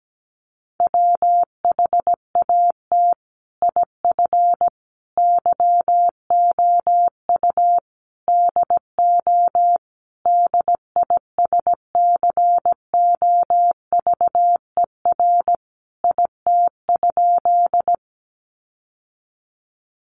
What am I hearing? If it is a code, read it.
Morse code, heard as a keyed tone, about 17 wpm, 700 Hz. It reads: WHAT IF YOU DO DISCOVER IT?
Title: Sentences from Top 1100 Words 17wpm